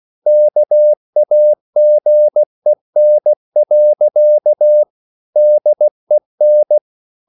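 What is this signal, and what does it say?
Morse code, heard as a keyed tone, about 16 wpm, 600 Hz. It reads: KAGEN. DEN
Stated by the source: Slow (QRS) Morse code at 16 wpm, from CW med Gnister.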